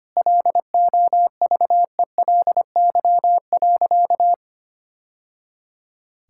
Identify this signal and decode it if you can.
Morse code, heard as a keyed tone, about 25 wpm, 700 Hz. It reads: LOVELY.
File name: practice / 25wpm / Sentences with 3-Word Phrases - 25wpm